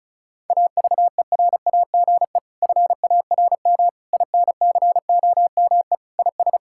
Morse code, heard as a keyed tone, about 36 wpm, 700 Hz. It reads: AVERAGE FARM INCOME IS
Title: Sentences from Top 1000 Words - Review Part 6 - 35wpm